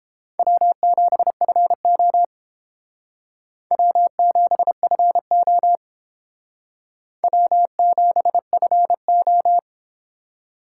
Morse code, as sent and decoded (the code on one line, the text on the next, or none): .-- --... ..-. --- / .-- --... ..-. --- / .-- --... ..-. ---
W7FO W7FO W7FO